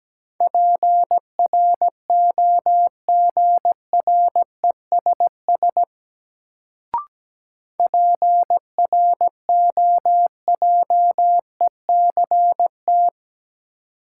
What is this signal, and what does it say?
Morse code, heard as a keyed tone, about 17 wpm, 700 Hz. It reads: PROGRESS PROJECT
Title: Sentences from Top 1400 Words - New Word Drill Alt - 17wpm